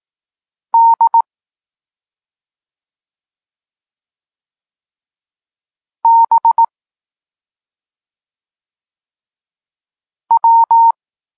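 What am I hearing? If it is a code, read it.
Morse code, heard as a keyed tone, about 18 wpm, 920 Hz. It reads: D B W
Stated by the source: Morse code lesson for learning the letters K and X, from the SP9MOA Niepołomicki Klub Krótkofalowców course.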